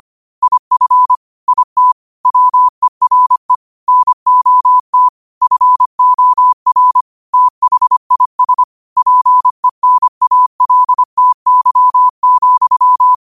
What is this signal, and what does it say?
Morse code, heard as a keyed tone, about 25 wpm, 1000 Hz. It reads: IF IT WERE NOT FOR THIS PENALTY,
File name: 25wpm